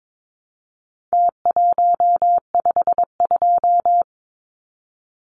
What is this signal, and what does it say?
Morse code, heard as a keyed tone, about 22 wpm, 700 Hz. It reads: T152
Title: Numbers - Contest Style - 4-Digit 22wpm